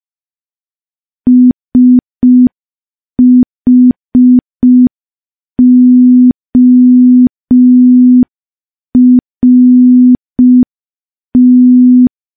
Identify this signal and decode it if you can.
Morse code, heard as a keyed tone, about 5 wpm, 250 Hz. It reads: SHORT
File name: morse / morse-training-05WPM